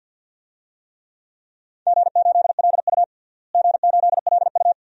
Morse code, heard as a keyed tone, about 50 wpm, 700 Hz. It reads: G8LU G8LU
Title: DXCC-100 Call Signs - 4 Characters - Repeated 2x - 50wpm